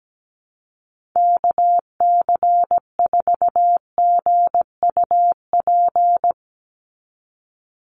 Morse code, heard as a keyed tone, about 17 wpm, 700 Hz. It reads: KC4GUP
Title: USA Call Signs 17wpm